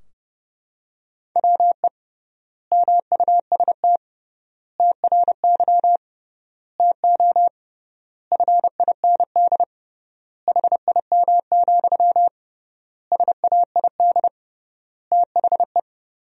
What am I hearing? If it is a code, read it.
Morse code, heard as a keyed tone, about 30 wpm, 700 Hz. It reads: WE MUST TRY TO FIND HIM, SAID THE
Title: 3x Word-Spacing - Sentences from Top 400 Words - 30wpm